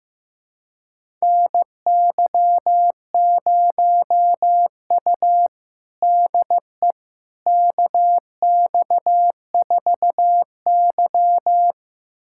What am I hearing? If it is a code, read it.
Morse code, heard as a keyed tone, about 15 wpm, 700 Hz. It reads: NY0U DE KX4Y
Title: USA Callsign DE Callsign - 4 Characters - 15wpm